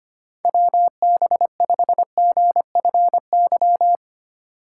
Morse code, heard as a keyed tone, about 25 wpm, 700 Hz. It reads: WB5GFY
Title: USA Call Signs - 6 Characters - 3x Character Spacing - 25wpm